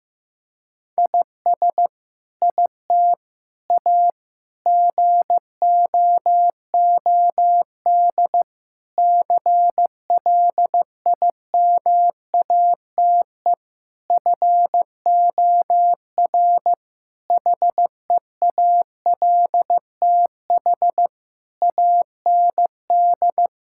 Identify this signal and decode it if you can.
Morse code, heard as a keyed tone, about 15 wpm, 700 Hz. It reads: IS IT A GOOD CLIMATE FOR HEALTH AND
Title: Sentences from Top 1600 Words 15wpm